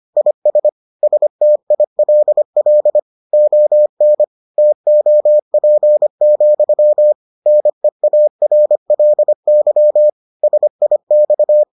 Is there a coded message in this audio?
Yes: Morse code from 0.2 to 11.6 s, heard as a keyed tone, about 25 wpm, 600 Hz. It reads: IS STILL ON TOP, NEARLY SIX